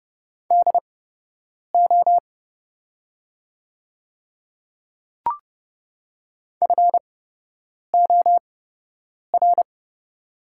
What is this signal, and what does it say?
Morse code, heard as a keyed tone, about 32 wpm, spaced as for 8 wpm, 700 Hz. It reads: DO FOR
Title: 8x Character Spacing - Top 100 Words - 30wpm